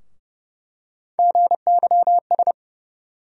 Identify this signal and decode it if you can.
Morse code, heard as a keyed tone, about 30 wpm, 700 Hz. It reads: GYS
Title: USA Call Sign Postfixes - 3 Letters 30wpm